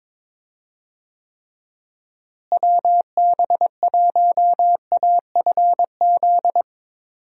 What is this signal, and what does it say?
Morse code, heard as a keyed tone, about 22 wpm, 700 Hz. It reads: WB1AFZ
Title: USA Call Signs - 6 Characters - Repeated 3x - 22wpm